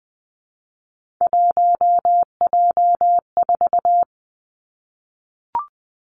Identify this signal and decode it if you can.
Morse code, heard as a keyed tone, about 20 wpm, 700 Hz. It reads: 1J4
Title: Three Challenging Letter-Number Combinations 20wpm